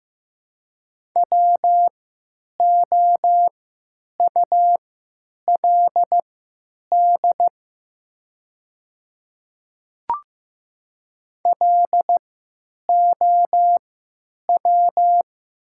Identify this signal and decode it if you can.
Morse code, heard as a keyed tone, about 15 wpm, spaced as for 9 wpm, 700 Hz. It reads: WOULD LOW
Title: 3x Character Spacing - Top 500 Words - 15wpm